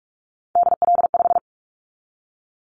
Morse code, heard as a keyed tone, about 44 wpm, 700 Hz. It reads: DL5
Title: Callsign Prefixes for Top 50 DXCC - 45wpm